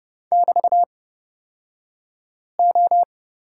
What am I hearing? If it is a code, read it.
Morse code, heard as a keyed tone, about 30 wpm, spaced as for 5 wpm, 700 Hz. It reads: <BT>O
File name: Lesson 37d - Letter x - Two Characters 30wpm